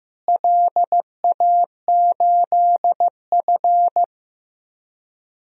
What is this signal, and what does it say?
Morse code, heard as a keyed tone, about 15 wpm, 700 Hz. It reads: LA8F
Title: DXCC-100 Call Signs - 4 Characters - Repeated 3x - 15wpm